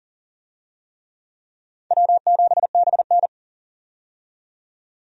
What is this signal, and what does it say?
Morse code, heard as a keyed tone, about 38 wpm, 700 Hz. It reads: W7BN
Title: USA Call Signs - 4 Characters - 40wpm